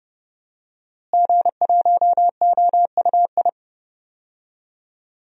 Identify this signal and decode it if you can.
Morse code, heard as a keyed tone, about 31 wpm, 700 Hz. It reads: G1OUI
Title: DXCC-100 Call Signs - 5 Characters - Repeated 2x - 30wpm